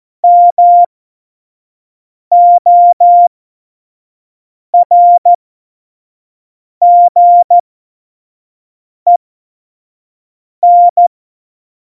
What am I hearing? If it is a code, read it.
Morse code, heard as a keyed tone, about 14 wpm, spaced as for 5 wpm, 700 Hz. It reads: MORGEN
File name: Morsekurs / Witze / Witz 15. September 2020, 25 F70